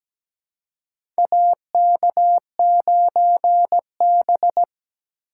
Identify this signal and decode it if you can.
Morse code, heard as a keyed tone, about 17 wpm, 700 Hz. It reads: AK9B
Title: USA Call Signs - 4 Characters - 4x Character Spacing - 17wpm